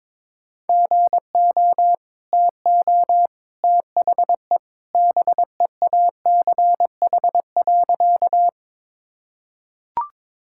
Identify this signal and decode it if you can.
Morse code, heard as a keyed tone, about 22 wpm, 700 Hz. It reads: GO TO THE BEACH.
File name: practice / 22wpm / Easy Sentences with Hints - 22wpm